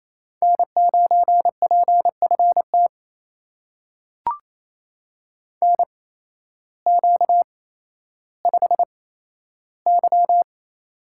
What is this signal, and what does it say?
Morse code, heard as a keyed tone, about 28 wpm, spaced as for 13 wpm, 700 Hz. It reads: N9PFT NQ5Y
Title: USA Call Signs - 8x Character Spacing - 28wpm